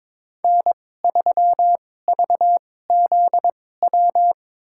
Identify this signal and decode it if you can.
Morse code, heard as a keyed tone, about 21 wpm, spaced as for 16 wpm, 700 Hz. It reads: N3VZW
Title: USA Call Signs - 2x Character Spacing - 22wpm